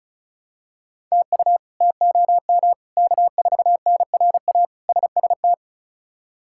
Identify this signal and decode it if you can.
Morse code, heard as a keyed tone, about 36 wpm, 700 Hz. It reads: TU TOM K4NRA SST
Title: K1USN SST QSO Practice - 35wpm